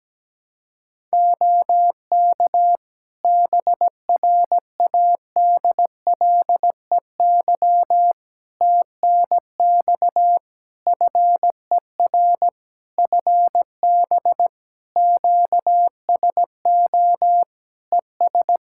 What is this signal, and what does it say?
Morse code, heard as a keyed tone, about 17 wpm, 700 Hz. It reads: OK BRADLEY TNX FER FB QSO ES
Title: Practice QSO - Exchange 4 - Ending - 17wpm